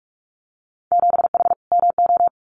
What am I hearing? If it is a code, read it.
Morse code, heard as a keyed tone, about 46 wpm, 690 Hz. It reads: 7H MO
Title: Winter Field Day Exchanges Encore 45wpm